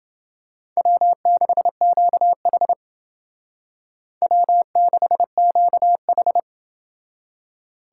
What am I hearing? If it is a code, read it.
Morse code, heard as a keyed tone, about 29 wpm, 700 Hz. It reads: W6QH W6QH